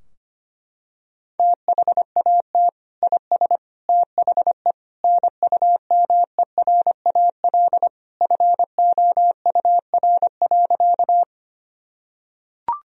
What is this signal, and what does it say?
Morse code, heard as a keyed tone, about 25 wpm, 700 Hz. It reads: THAT IS THE NUMERAL FOUR.
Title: Sentences from Top 1000 Words - Review Part 2 - 25wpm